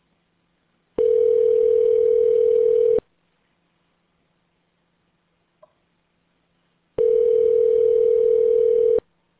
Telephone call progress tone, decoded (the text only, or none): ringback tone